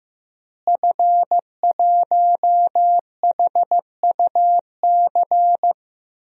Morse code, heard as a keyed tone, about 16 wpm, 700 Hz. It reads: F1HUC
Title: DXCC-100 Call Signs - 5 Characters - Repeated 3x - 15wpm